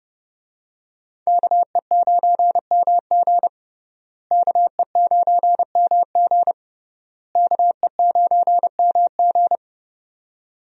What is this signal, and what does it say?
Morse code, heard as a keyed tone, about 30 wpm, 700 Hz. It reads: KE9MG KE9MG KE9MG
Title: USA Call Signs - 5 Characters - Repeated 3x - 30wpm